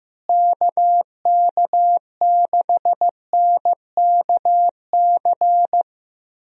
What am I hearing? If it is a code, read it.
Morse code, heard as a keyed tone, about 15 wpm, 700 Hz. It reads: KK6NKC
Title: USA Call Signs - 6 Characters - Repeated 3x - 15wpm